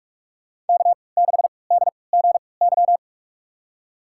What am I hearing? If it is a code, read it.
Morse code, heard as a keyed tone, about 46 wpm, spaced as for 26 wpm, 700 Hz. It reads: K6DGY
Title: USA Call Signs - 5 Characters - 3x Character Spacing - 45wpm